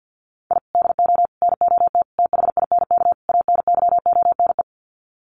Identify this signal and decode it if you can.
Morse code, heard as a keyed tone, about 50 wpm, 700 Hz. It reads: I DO NOT THINK ANYONE
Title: Sets of 5 Words Encore 50wpm